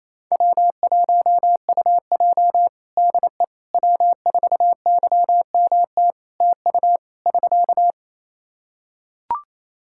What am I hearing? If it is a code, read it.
Morse code, heard as a keyed tone, about 28 wpm, 700 Hz. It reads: W1UJ DE W4YMT TU <SK>
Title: Practice QSO - Exchange 4 - Ending - 28wpm